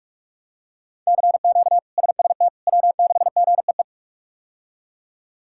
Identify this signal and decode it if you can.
Morse code, heard as a keyed tone, about 41 wpm, 700 Hz. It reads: CQ SST W6GEE
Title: K1USN SST QSO Practice - 45wpm